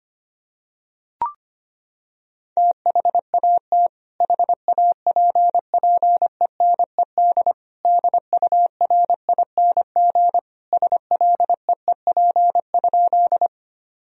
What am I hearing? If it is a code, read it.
Morse code, heard as a keyed tone, about 25 wpm, 700 Hz. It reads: THAT HAPPENED DURING SLEEP?
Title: Sentences from Top 1000 Words - Review Part 2 - 25wpm